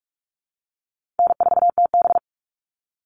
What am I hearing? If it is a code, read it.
Morse code, heard as a keyed tone, about 44 wpm, 700 Hz. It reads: N4TB
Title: Easy Call Signs - 45wpm